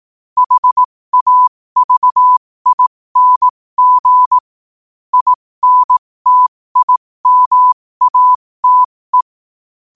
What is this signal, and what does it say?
Morse code, heard as a keyed tone, about 18 wpm, spaced as for 15 wpm, 1000 Hz. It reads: HAVING INTIMATE
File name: qotdfiles / 15wpm